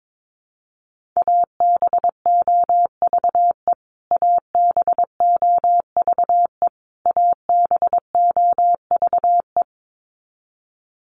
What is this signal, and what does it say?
Morse code, heard as a keyed tone, about 22 wpm, 700 Hz. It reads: ABOVE ABOVE ABOVE
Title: Top 500 Words Repeated 22wpm